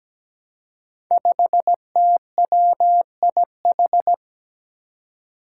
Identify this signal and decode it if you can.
Morse code, heard as a keyed tone, about 17 wpm, 700 Hz. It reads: 5TWIH
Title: Five Challenging Letter-Number Combinations 17wpm